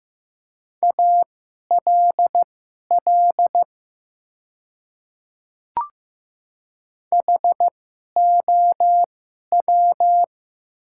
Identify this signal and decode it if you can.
Morse code, heard as a keyed tone, about 15 wpm, spaced as for 11 wpm, 700 Hz. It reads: ALL HOW
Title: 2x Character Spacing - Top 100 Words - 15wpm